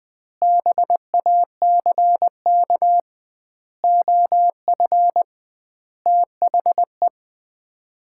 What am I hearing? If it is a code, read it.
Morse code, heard as a keyed tone, about 20 wpm, 700 Hz. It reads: BACK OF THE
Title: 2x Word-Spacing - Sets of 5 Words - 20wpm